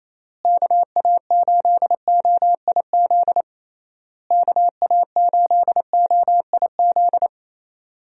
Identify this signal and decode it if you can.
Morse code, heard as a keyed tone, about 28 wpm, 700 Hz. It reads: KA8OIZ KA8OIZ
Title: USA Call Signs - 6 Characters - Repeated 3x - 28wpm